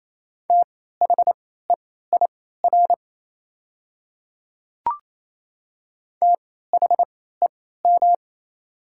Morse code, heard as a keyed tone, about 28 wpm, spaced as for 16 wpm, 690 Hz. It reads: THEIR THEM
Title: 3x Character Spacing - Top 100 Words - 28wpm